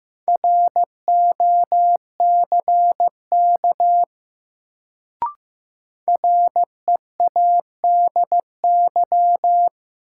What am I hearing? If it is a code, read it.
Morse code, heard as a keyed tone, about 15 wpm, 700 Hz. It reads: ROCK READY
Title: Sentences from Top 400 Words - New Word Drill - 15wpm